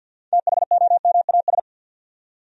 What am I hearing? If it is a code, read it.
Morse code, heard as a keyed tone, about 53 wpm, 700 Hz. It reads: THOMAS
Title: USA Common Names - 50wpm